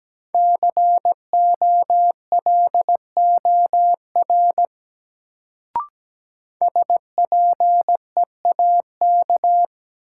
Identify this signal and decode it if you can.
Morse code, heard as a keyed tone, about 17 wpm, 700 Hz. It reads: COLOR SPEAK
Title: Sentences from Top 1000 Words - Review Part 2 - New Word Drill - 17wpm